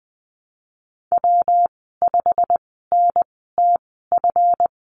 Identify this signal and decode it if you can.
Morse code, heard as a keyed tone, about 20 wpm, spaced as for 15 wpm, 700 Hz. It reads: W5NTF
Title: 213 Lesson 34g - Number 8 - Call Signs 20wpm